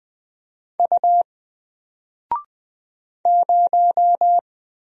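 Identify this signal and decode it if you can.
Morse code, heard as a keyed tone, about 20 wpm, 700 Hz. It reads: U 0
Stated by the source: Single letter and number Morse drill at 20 wpm, from N4LCD.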